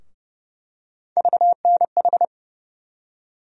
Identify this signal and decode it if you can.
Morse code, heard as a keyed tone, about 30 wpm, 700 Hz. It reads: VNH
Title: USA Call Sign Postfixes - 3 Letters 30wpm